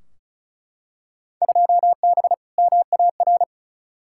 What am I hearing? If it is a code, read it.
Morse code, heard as a keyed tone, about 34 wpm, 700 Hz. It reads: 2B MAR
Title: Field Day Exchanges Encore - 35wpm